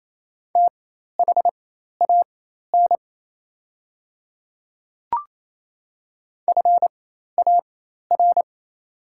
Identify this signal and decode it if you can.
Morse code, heard as a keyed tone, about 28 wpm, spaced as for 13 wpm, 700 Hz. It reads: THAN FAR
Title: 4x Character Spacing - Top 300 Words - 28wpm